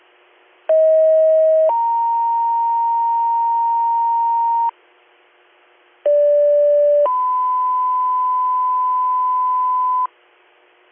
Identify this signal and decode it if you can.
Two-tone page, heard as a one-second tone, then a three-second tone, about 640 then 930 Hz, about 580 then 1010 Hz. It is heard from 0.7 to 10.1 s.